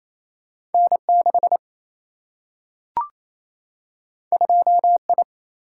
Morse code, heard as a keyed tone, about 28 wpm, 700 Hz. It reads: N 6 2 I
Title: Single-Letter Single-Number 28wpm